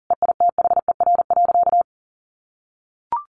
Morse code, heard as a keyed tone, about 40 wpm, 700 Hz. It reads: EITHER.
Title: Sentences from Top 1200 Words 40wpm